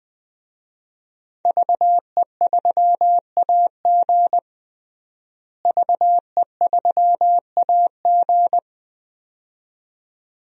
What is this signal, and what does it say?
Morse code, heard as a keyed tone, about 21 wpm, 700 Hz. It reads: VE3AG VE3AG
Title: DXCC-100 Call Signs - 5 Characters - Repeated 2x - 20wpm